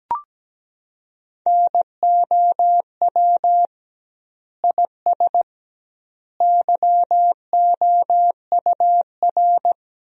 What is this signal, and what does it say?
Morse code, heard as a keyed tone, about 17 wpm, 700 Hz. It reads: NOW IS YOUR
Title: 2x Word-Spacing - Sentences from Top 100 Words - 17wpm